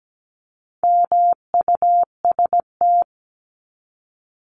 Morse code, heard as a keyed tone, about 17 wpm, 700 Hz. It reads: MUST
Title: Top 500 Words 17wpm